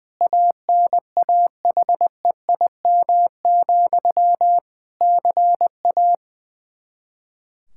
Morse code, heard as a keyed tone, about 20 wpm, 700 Hz. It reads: ANAHEIM, CA
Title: Practice QSO - Exchange 1 - Beginning - 20wpm